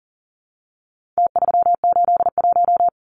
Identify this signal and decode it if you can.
Morse code, heard as a keyed tone, about 40 wpm, 700 Hz. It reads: T381
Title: Numbers - Contest Style - 4-Digit 40wpm